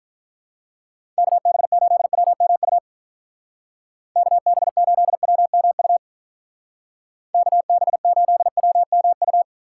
Morse code, heard as a keyed tone, about 45 wpm, 700 Hz. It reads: KB8WMU KB8WMU KB8WMU